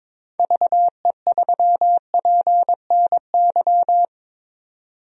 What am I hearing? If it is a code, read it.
Morse code, heard as a keyed tone, about 21 wpm, 700 Hz. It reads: VE3PNY
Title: DXCC-100 Call Signs - 6 Characters - 22wpm